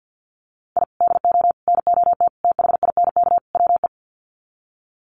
Morse code, heard as a keyed tone, about 50 wpm, 700 Hz. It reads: I DO NOT THINK WE
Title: Sets of 5 Words 50wpm